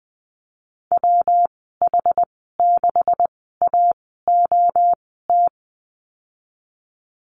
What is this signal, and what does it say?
Morse code, heard as a keyed tone, about 20 wpm, spaced as for 15 wpm, 700 Hz. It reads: WH6AOT
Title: 200 Lesson 32g - Letter k - Call Signs 20wpm